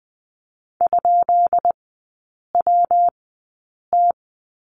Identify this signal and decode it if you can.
Morse code, heard as a keyed tone, about 20 wpm, spaced as for 8 wpm, 700 Hz. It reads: ?WT
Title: 152 Lesson 25e - Letter p - Three Characters 20wpm